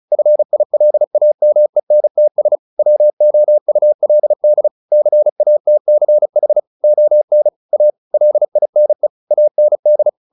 Morse code, heard as a keyed tone, about 34 wpm, 600 Hz. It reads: FILAMENTS WOULD CATCH ON A LINE AND
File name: the old man and the sea / the old man and the sea 1 35wpm